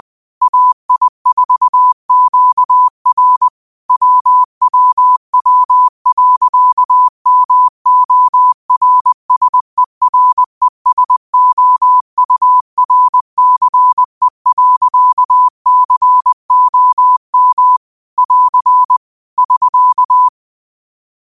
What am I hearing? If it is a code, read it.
Morse code, heard as a keyed tone, about 20 wpm, 1000 Hz. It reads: AI4QR WWW.MORSERESOURCE.COM <AR> <SK>
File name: qotdfiles / 20wpm